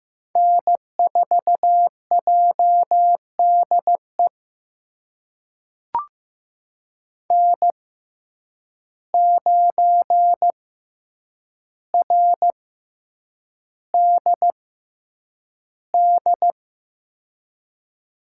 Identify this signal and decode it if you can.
Morse code, heard as a keyed tone, about 15 wpm, spaced as for 8 wpm, 700 Hz. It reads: N4JDE N9RDD
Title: USA Call Signs - 5 Characters - 6x Character Spacing - 15wpm